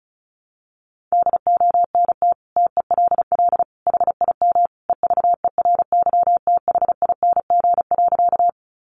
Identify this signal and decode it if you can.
Morse code, heard as a keyed tone, about 34 wpm, 700 Hz. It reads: DONT TELL HIM EVERYTHING.
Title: Sentences from Top 700 Words 35wpm